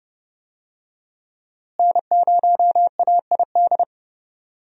Morse code, heard as a keyed tone, about 30 wpm, 700 Hz. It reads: N0AID
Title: USA Call Signs - 5 Characters - Repeated 2x - 30wpm